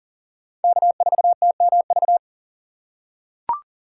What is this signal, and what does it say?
Morse code, heard as a keyed tone, about 40 wpm, 700 Hz. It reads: K4TMV
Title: USA Call Signs - 5 Characters - 3x Character Spacing - 40wpm